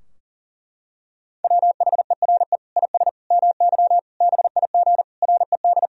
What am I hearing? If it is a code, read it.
Morse code, heard as a keyed tone, about 40 wpm, 700 Hz. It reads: WHERE IS MY BIG RED
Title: Sentences from Top 1000 Words - Review Part 2 - 40wpm